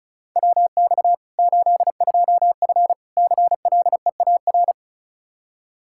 Morse code, heard as a keyed tone, about 36 wpm, 700 Hz. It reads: WX 82F CLEAR
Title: Practice QSO - Exchange 2 - Info - 35wpm